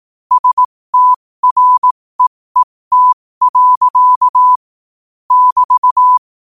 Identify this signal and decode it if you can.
Morse code, heard as a keyed tone, about 18 wpm, spaced as for 15 wpm, 1000 Hz. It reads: STREET. <BT>
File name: qotdfiles / 15wpm